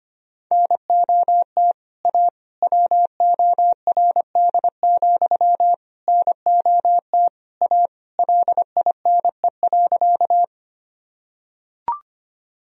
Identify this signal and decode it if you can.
Morse code, heard as a keyed tone, about 25 wpm, 700 Hz. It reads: NOT A WORD, NOT A LINE.
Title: Sentences from Top 1000 Words - Review Part 1 - 25wpm